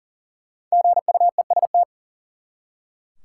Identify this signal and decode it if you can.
Morse code, heard as a keyed tone, about 40 wpm, 700 Hz. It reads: GUEST
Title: Five Letter Words - 40wpm